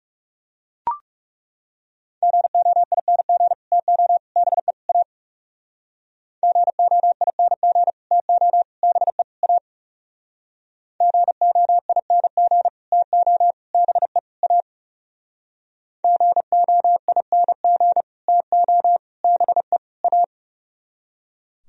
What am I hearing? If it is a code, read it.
Morse code, heard as a keyed tone, about 38 wpm, 700 Hz. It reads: GOING TO BE A GOING TO BE A GOING TO BE A GOING TO BE A
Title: Speed-Racing - Sets of 4 Words 30wpm